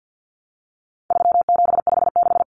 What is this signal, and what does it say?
Morse code, heard as a keyed tone, about 50 wpm, 700 Hz. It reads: 3756
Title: Common Cut Numbers - 4-Digit 50wpm